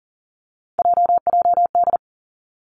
Morse code, heard as a keyed tone, about 38 wpm, 700 Hz. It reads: JJD
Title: Three Challenging Letter-Number Combinations 40wpm